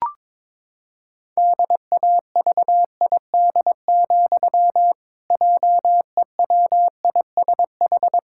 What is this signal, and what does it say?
Morse code, heard as a keyed tone, about 22 wpm, 700 Hz. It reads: DAVID, JEWISH
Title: Pangram Challenge v2 - 22wpm